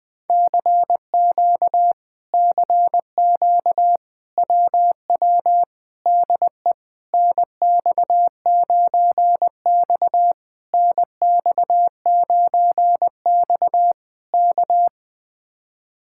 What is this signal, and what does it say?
Morse code, heard as a keyed tone, about 20 wpm, 700 Hz. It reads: CQ CQ WW DE NX9X NX9X K
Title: Speed-Racing Common QSO Phrases - 17wpm